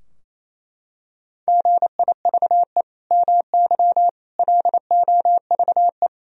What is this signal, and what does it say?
Morse code, heard as a keyed tone, about 28 wpm, 700 Hz. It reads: GIVE MY LOVE
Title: Sentences from Top 1000 Words - Review Part 4 - 28wpm